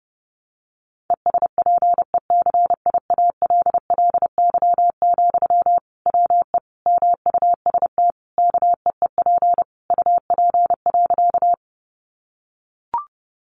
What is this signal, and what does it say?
Morse code, heard as a keyed tone, about 30 wpm, 700 Hz. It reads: ESPECIALLY, WE MUST KEEP UP.
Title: Sentences from Top 700 Words 30wpm